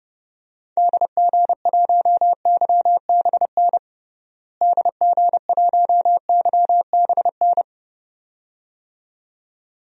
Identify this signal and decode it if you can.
Morse code, heard as a keyed tone, about 32 wpm, 700 Hz. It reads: DG1YBN DG1YBN
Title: DXCC-100 Call Signs - 6 Characters - Repeated 2x - 30wpm